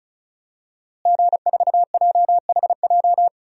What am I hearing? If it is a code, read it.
Morse code, heard as a keyed tone, about 36 wpm, 700 Hz. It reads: G4JHJ
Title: DXCC-100 Call Signs - 5 Characters - Repeated 3x - 35wpm